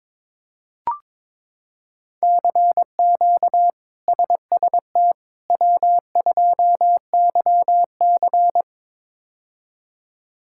Morse code, heard as a keyed tone, about 22 wpm, 700 Hz. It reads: CQ SST W2YC
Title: K1USN SST QSO Practice - 22wpm